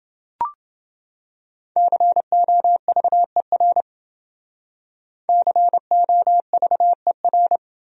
Morse code, heard as a keyed tone, about 30 wpm, 700 Hz. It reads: COVER COVER